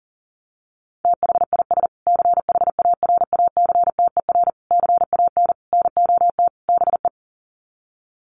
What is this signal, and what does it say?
Morse code, heard as a keyed tone, about 40 wpm, 700 Hz. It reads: THIS CHARACTER CAN NOT BE
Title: Sets of 5 Words Encore2 40wpm